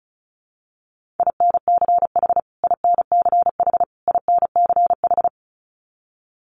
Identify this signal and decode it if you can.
Morse code, heard as a keyed tone, about 34 wpm, 700 Hz. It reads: INCH INCH INCH
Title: Top 500 Words Repeated 35wpm